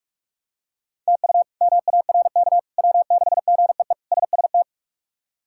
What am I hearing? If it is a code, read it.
Morse code, heard as a keyed tone, about 46 wpm, 700 Hz. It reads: TU MARK W6GEE SST
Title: K1USN SST QSO Practice - 45wpm